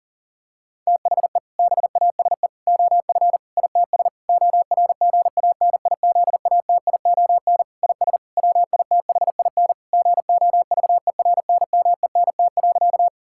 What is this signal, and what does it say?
Morse code, heard as a keyed tone, about 40 wpm, 700 Hz. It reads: THE BASE OF ITS ORGANIZATION IS WITHIN GOVERNMENT.
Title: Sentences from Top 1000 Words - Review Part 5 - 40wpm